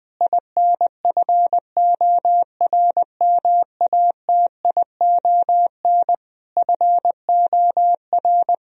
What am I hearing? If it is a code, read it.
Morse code, heard as a keyed tone, about 20 wpm, 700 Hz. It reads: INFORMATION FOR
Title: Sentences from Top 1000 Words - Review Part 3 - 20wpm